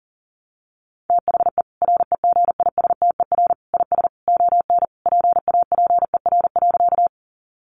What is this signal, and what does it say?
Morse code, heard as a keyed tone, about 40 wpm, 700 Hz. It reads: THE REGISTER IS ON PAPER.
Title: Sentences from Top 1200 Words 40wpm